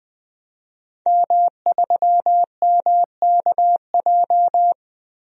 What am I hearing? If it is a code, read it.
Morse code, heard as a keyed tone, about 20 wpm, 700 Hz. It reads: M3MKJ
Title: DXCC-100 Call Signs - 5 Characters - Repeated 3x - 20wpm